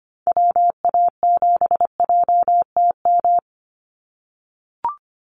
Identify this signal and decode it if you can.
Morse code, heard as a keyed tone, about 25 wpm, 700 Hz. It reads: WA7JTM
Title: NA SOTA Chaser Callsigns 25wpm